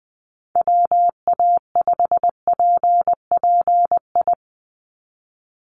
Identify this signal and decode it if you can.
Morse code, heard as a keyed tone, about 20 wpm, 700 Hz. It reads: WA5PPI
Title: USA Call Signs 20wpm